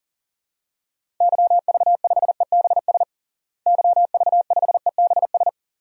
Morse code, heard as a keyed tone, about 40 wpm, 700 Hz. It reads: YV5EBS YV5EBS
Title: DXCC-100 Call Signs - 6 Characters - Repeated 2x - 40wpm